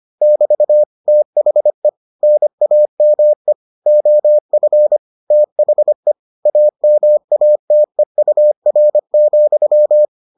Morse code, heard as a keyed tone, about 25 wpm, 600 Hz. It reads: <BT> THE NAME OF THE AMATEUR,